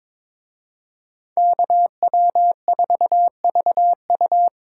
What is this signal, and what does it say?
Morse code, heard as a keyed tone, about 26 wpm, 700 Hz. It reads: KW4VU
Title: Speed Racing - USA 2x2 Call Signs 22wpm